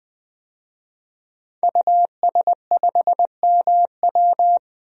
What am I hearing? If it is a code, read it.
Morse code, heard as a keyed tone, about 21 wpm, 700 Hz. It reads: US5MW
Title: DXCC-100 Call Signs - 5 Characters - 20wpm